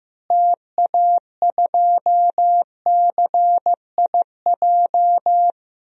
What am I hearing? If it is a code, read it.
Morse code, heard as a keyed tone, about 16 wpm, 700 Hz. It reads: TA2CIJ
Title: DXCC-100 Call Signs - 6 Characters - Repeated 3x - 15wpm